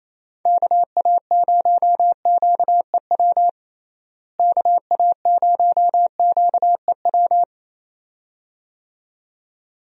Morse code, heard as a keyed tone, about 28 wpm, 700 Hz. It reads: KA0QEW KA0QEW
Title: USA Call Signs - 6 Characters - Repeated 2x - 28wpm